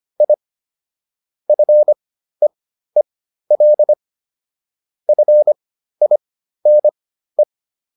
Morse code, heard as a keyed tone, about 25 wpm, spaced as for 13 wpm, 600 Hz. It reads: I FEEL FINE